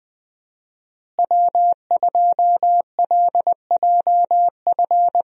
Morse code, heard as a keyed tone, about 20 wpm, 700 Hz. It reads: W2LJF
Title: USA Call Signs - 5 Characters - Repeated 2x - 20wpm